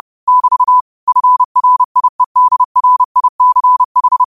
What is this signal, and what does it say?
Morse code, heard as a keyed tone, about 30 wpm, 1000 Hz. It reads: <BT> FRIEDRICH